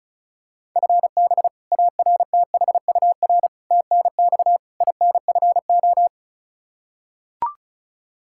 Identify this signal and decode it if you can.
Morse code, heard as a keyed tone, about 34 wpm, 700 Hz. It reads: FB ARTHUR TNX INFO
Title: Practice QSO - Exchange 3 - More Info - 35wpm